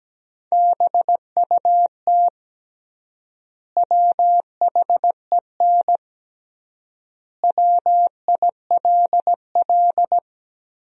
Morse code, heard as a keyed tone, about 17 wpm, 700 Hz. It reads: BUT WHEN WILL
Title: 3x Word-Spacing - Sentences from Top 100 Words - 17wpm